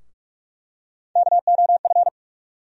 Morse code, heard as a keyed tone, about 43 wpm, 700 Hz. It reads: KOF
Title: USA Call Sign Postfixes - 3 Letters 45wpm